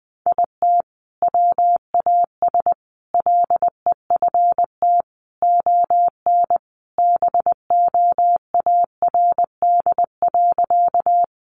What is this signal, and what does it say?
Morse code, heard as a keyed tone, about 20 wpm, 700 Hz. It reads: IT WAS LEFT ON BOARD.
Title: Sentences from Top 700 Words 20wpm